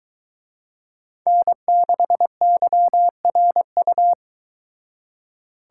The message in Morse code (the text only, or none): N6YRU